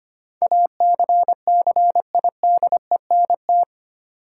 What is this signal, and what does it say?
Morse code, heard as a keyed tone, about 25 wpm, 700 Hz. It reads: ACCIDENT
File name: 8 Letter Words - First 3 Letters Given - 25wpm